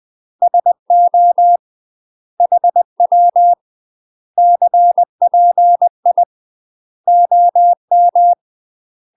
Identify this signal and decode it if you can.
Morse code, heard as a keyed tone, about 20 wpm, 700 Hz. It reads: SO HW CPI OM
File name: UK MASTER QSOs TRAINING / 20WPM / UK MASTER QSOs 0020